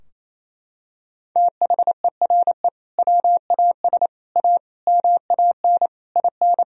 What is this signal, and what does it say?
Morse code, heard as a keyed tone, about 28 wpm, 700 Hz. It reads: THERE WAS A MAN IN
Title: Sentences from Top 1000 Words - Review Part 3 - 28wpm